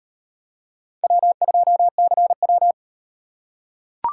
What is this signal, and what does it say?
Morse code, heard as a keyed tone, about 37 wpm, 700 Hz. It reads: W2CW